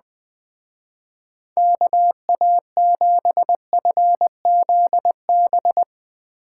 Morse code, heard as a keyed tone, about 23 wpm, 700 Hz. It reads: KA7FZB
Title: Speed Racing - USA Call Signs 20wpm